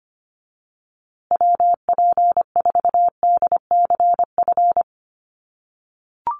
Morse code, heard as a keyed tone, about 25 wpm, 700 Hz. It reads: WP4DCF
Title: USA Call Signs Repeated 2x 25wpm